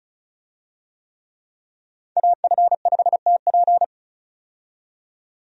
Morse code, heard as a keyed tone, about 36 wpm, 700 Hz. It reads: AF5TP
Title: USA Call Signs - 5 Characters - 35wpm